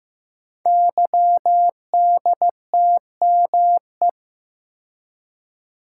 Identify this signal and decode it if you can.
Morse code, heard as a keyed tone, about 15 wpm, 700 Hz. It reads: YDTME
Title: Code Groups - 5 Letters - 15wpm